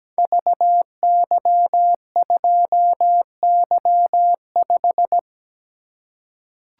Morse code, heard as a keyed tone, about 17 wpm, 700 Hz. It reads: VY2Y5
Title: Code Groups - 5 Letter:Numbers - 17wpm